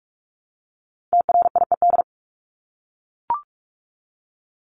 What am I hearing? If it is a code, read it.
Morse code, heard as a keyed tone, about 44 wpm, 700 Hz. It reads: TRIED
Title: ED Suffix - Top 100 Words with Base Words 45wpm